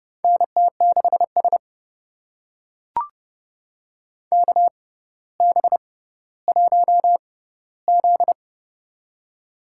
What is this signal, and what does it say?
Morse code, heard as a keyed tone, about 32 wpm, spaced as for 16 wpm, 700 Hz. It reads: NT6S KB1Z
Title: USA Call Signs - 4 Characters - 6x Character Spacing - 30wpm